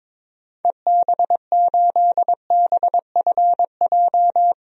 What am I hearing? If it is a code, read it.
Morse code, heard as a keyed tone, about 22 wpm, 700 Hz. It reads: EB8BFJ
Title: DXCC-100 Call Signs - 6 Characters - 22wpm